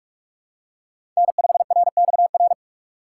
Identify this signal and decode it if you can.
Morse code, heard as a keyed tone, about 44 wpm, 700 Hz. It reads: N5RXR
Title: USA Call Signs - 6x Character Spacing - 45wpm